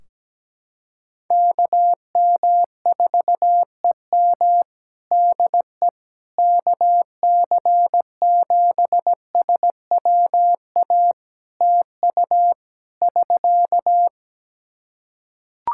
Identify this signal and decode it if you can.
Morse code, heard as a keyed tone, about 17 wpm, 700 Hz. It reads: KM4EM DE KC7SWA TU <SK>
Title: Practice QSO - Exchange 4 - Ending - 17wpm